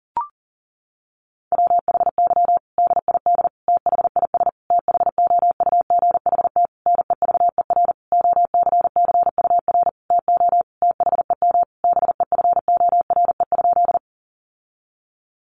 Morse code, heard as a keyed tone, about 40 wpm, 700 Hz. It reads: WHY DID THIS THOUGHT NEVER OCCUR TO THEM BEFORE?
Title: Sentences from Top 900 Words 40wpm